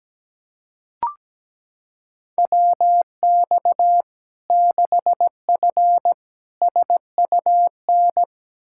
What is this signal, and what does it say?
Morse code, heard as a keyed tone, about 17 wpm, 700 Hz. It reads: WX 6F SUN
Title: Practice QSO - Exchange 2 - Info - 17wpm